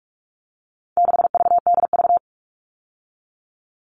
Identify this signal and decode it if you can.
Morse code, heard as a keyed tone, about 44 wpm, 700 Hz. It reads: 6VDV
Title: All Cut Numbers - 4-Digit 45wpm